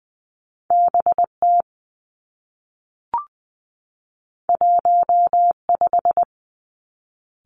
Morse code, heard as a keyed tone, about 20 wpm, 700 Hz. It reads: B T 1 5
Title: Two Challenging Letter-Number Combinations 20wpm